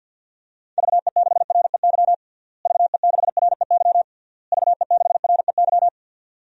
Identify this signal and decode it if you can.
Morse code, heard as a keyed tone, about 49 wpm, 700 Hz. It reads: VE6REY VE6REY VE6REY
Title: DXCC-100 Call Signs - 6 Characters - Repeated 3x - 50wpm